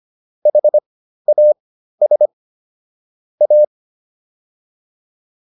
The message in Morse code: HAS A

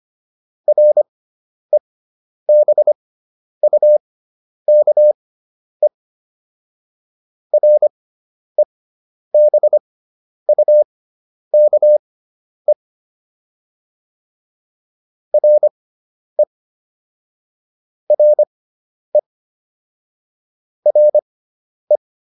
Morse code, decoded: REBUKE REBUKE RE RE RE

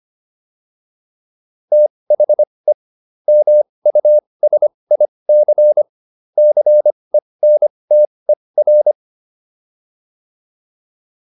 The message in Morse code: THE MUSIC CENTER

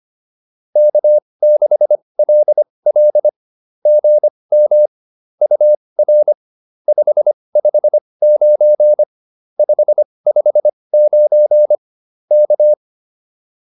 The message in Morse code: K6LL GM UR 559 559 K

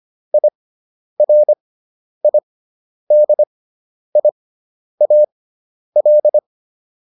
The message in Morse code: IRIDIAL

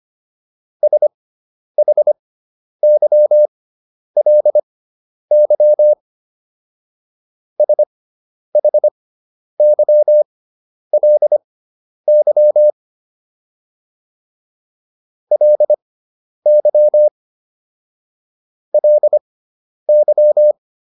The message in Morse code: SHYLY SHYLY LY LY

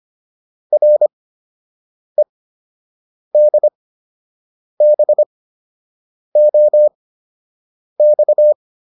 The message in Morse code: REDBOX

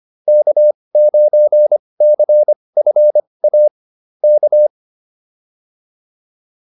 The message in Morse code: K9CFA K

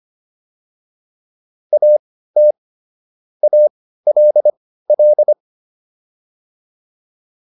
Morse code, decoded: AT ALL

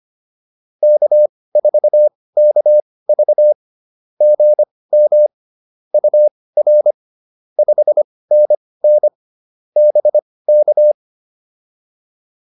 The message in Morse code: K4KV GM UR 5NN BK

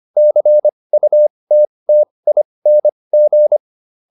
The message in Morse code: CUTTING

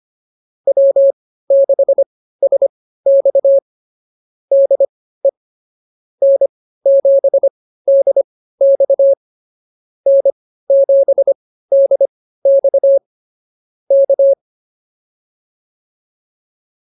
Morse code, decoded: W6SX DE N7DX N7DX K

